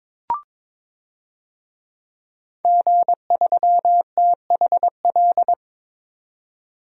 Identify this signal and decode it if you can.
Morse code, heard as a keyed tone, about 22 wpm, 700 Hz. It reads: G3THL
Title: DXCC-100 Call Signs - 5 Characters - 22wpm